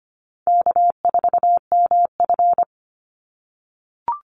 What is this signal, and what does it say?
Morse code, heard as a keyed tone, about 25 wpm, 700 Hz. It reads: K4MF
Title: NA SOTA Chaser Callsigns 25wpm